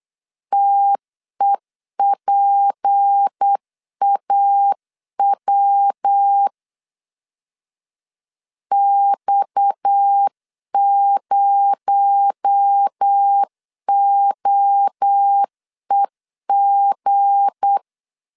Morse code, heard as a keyed tone, about 8 wpm, 800 Hz. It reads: TEPAW X0OEG
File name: LEZIONI / 7 GF9Z / 7 G 8wpm 3118